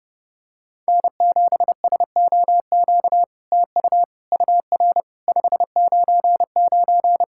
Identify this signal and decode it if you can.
Morse code, heard as a keyed tone, about 30 wpm, 700 Hz. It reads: N7SOQ TU UR 599
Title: SOTA QSO Practice 30wpm